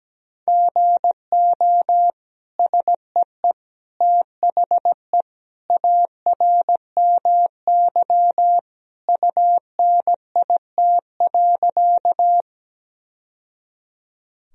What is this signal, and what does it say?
Morse code, heard as a keyed tone, about 17 wpm, 700 Hz. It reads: GO SEE THE ARMY UNIT.
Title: Sentences from Top 1000 Words - Review Part 2 - 17wpm